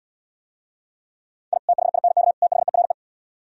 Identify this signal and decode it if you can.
Morse code, heard as a keyed tone, about 76 wpm, 700 Hz. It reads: I THINK THERE